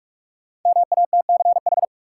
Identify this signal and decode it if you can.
Morse code, heard as a keyed tone, about 46 wpm, 700 Hz. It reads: MATCH
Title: Sentences from Top 1600 Words - New Word Drill - 45wpm